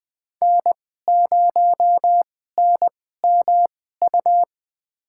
Morse code, heard as a keyed tone, about 20 wpm, spaced as for 14 wpm, 700 Hz. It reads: N0NMU
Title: USA Call Signs - 5 Characters - 2x Character Spacing - 20wpm